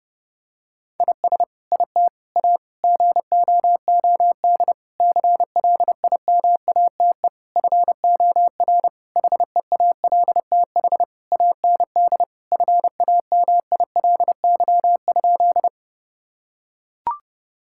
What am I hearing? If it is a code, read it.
Morse code, heard as a keyed tone, about 30 wpm, 700 Hz. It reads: IS IT A GOOD CLIMATE FOR HEALTH AND FAMILY?
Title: Sentences from Top 1600 Words 30wpm